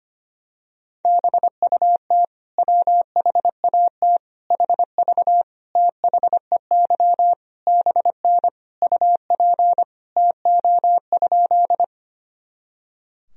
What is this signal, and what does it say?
Morse code, heard as a keyed tone, about 25 wpm, 700 Hz. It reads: BUT WHAT HV THEY BN UP TO?
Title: Sentences from Top 100 Words and Abbreviations 25wpm